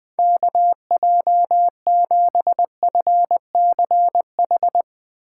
Morse code, heard as a keyed tone, about 21 wpm, 700 Hz. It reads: KJ7FCH